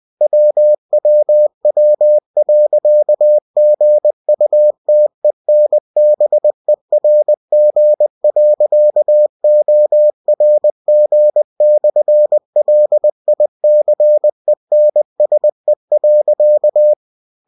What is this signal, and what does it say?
Morse code, heard as a keyed tone, about 20 wpm, 600 Hz. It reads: WWW.GUTENBERG.ORG/LICENSE.
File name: cw-books / winnie-the-pooh / winnie-20wpm-page-0113